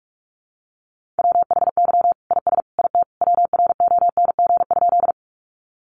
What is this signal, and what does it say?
Morse code, heard as a keyed tone, about 44 wpm, 700 Hz. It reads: WHY IS IT WRONG?
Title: Sentences from Top 900 Words 45wpm